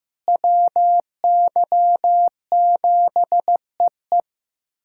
Morse code, heard as a keyed tone, about 15 wpm, 700 Hz. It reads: WY7EE
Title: USA Callsign DE Callsign - 5 Characters - 15wpm